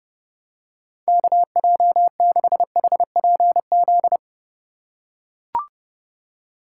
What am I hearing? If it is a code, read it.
Morse code, heard as a keyed tone, about 31 wpm, 700 Hz. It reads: KJ6HPZ
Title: USA Call Signs - 6 Characters - 4x Character Spacing - 30wpm